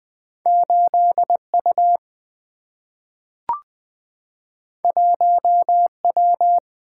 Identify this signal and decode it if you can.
Morse code, heard as a keyed tone, about 20 wpm, 700 Hz. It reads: 8U 1W
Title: Two Letter-Number Combinations - 20wpm